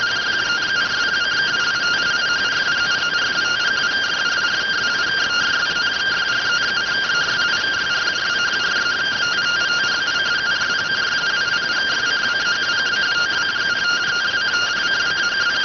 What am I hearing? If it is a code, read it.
Two-tone teleprinter signal, about 170 Hz shift.